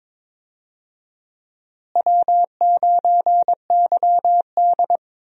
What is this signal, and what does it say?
Morse code, heard as a keyed tone, about 22 wpm, 700 Hz. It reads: W9YD
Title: USA Call Signs - 4 Characters - Repeated 3x - 22wpm